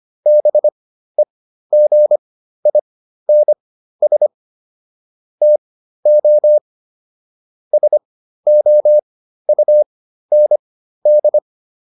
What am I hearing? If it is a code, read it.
Morse code, heard as a keyed tone, about 25 wpm, spaced as for 13 wpm, 600 Hz. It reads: BEGINS TO SOUND